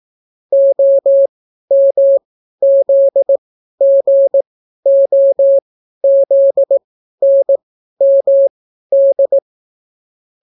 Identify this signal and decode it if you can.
Morse code, heard as a keyed tone, about 18 wpm, spaced as for 12 wpm, 550 Hz. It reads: OMZGOZNMD